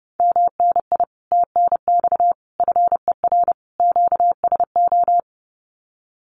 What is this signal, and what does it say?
Morse code, heard as a keyed tone, about 32 wpm, 700 Hz. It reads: MNI TNX FER QSO
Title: Common QSO Phrases Encore 30wpm